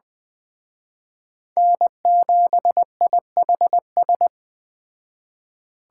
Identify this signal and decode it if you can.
Morse code, heard as a keyed tone, about 21 wpm, 700 Hz. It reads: N7IHS